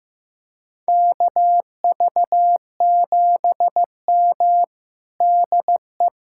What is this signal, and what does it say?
Morse code, heard as a keyed tone, about 16 wpm, 700 Hz. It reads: KV7M DE
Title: USA Callsign DE Callsign - 4 Characters - 15wpm